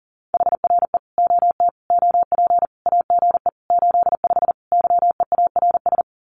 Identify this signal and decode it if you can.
Morse code, heard as a keyed tone, about 40 wpm, 700 Hz. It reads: HRE OT OP AGE 85 YEARS
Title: Common QSO Phrases 40wpm